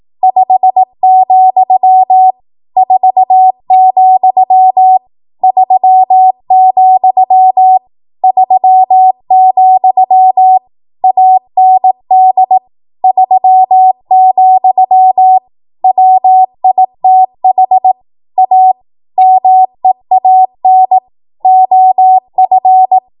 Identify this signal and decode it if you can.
Morse code, heard as a keyed tone, about 18 wpm, 750 Hz. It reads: 5, 4, 3, 3, AND 3, WITH A MEAN OF
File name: Morse / 18WPM